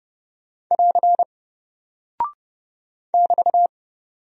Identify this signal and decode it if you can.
Morse code, heard as a keyed tone, about 29 wpm, 700 Hz. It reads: <AR> <BT>
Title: Single Character (Letter:Number:Punctuation:Prosign) 30wpm